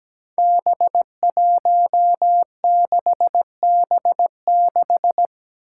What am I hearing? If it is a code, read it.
Morse code, heard as a keyed tone, about 17 wpm, 700 Hz. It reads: B16B6
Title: Five Challenging Letter-Number Combinations 17wpm